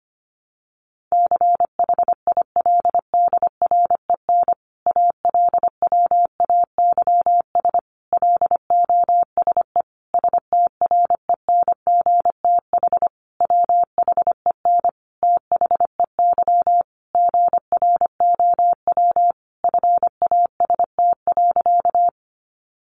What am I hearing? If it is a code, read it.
Morse code, heard as a keyed tone, about 25 wpm, 700 Hz. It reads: CHILDREN ALWAYS LOSE STRENGTH WHEN THEY GROW FAST.
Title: Sentences from Top 1300 Words 25wpm